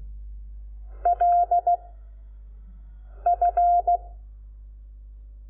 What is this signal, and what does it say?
Morse code, heard as a keyed tone, about 16 wpm, 700 Hz. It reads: L F
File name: Lecciones / CLASE Nº5